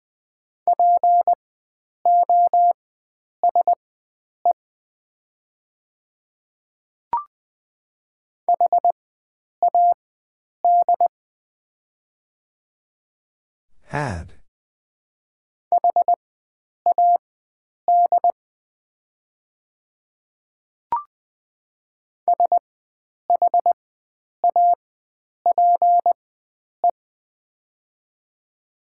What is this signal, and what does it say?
Morse code, heard as a keyed tone, about 20 wpm, spaced as for 9 wpm, 700 Hz. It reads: POSE HAD HAD SHAPE
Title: 4x Character Spacing - Top 500 Words - 20wpm